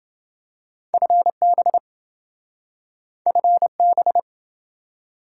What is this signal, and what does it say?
Morse code, heard as a keyed tone, about 28 wpm, 700 Hz. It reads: FB FB